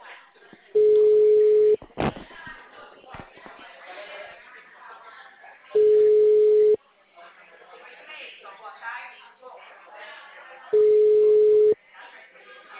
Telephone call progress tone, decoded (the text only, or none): ringback tone